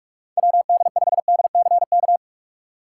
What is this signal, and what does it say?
Morse code, heard as a keyed tone, about 44 wpm, 700 Hz. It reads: WD5DCK
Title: USA Call Signs - 6 Characters - 6x Character Spacing - 45wpm